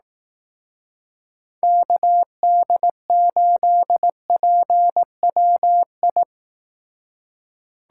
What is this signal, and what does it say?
Morse code, heard as a keyed tone, about 19 wpm, 700 Hz. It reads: KD8PWI